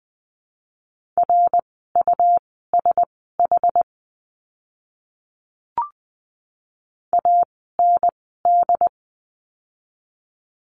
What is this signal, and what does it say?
Morse code, heard as a keyed tone, about 20 wpm, spaced as for 14 wpm, 700 Hz. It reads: RUSH AND
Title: 101 Lesson 17f - Letter u - Words 20wpm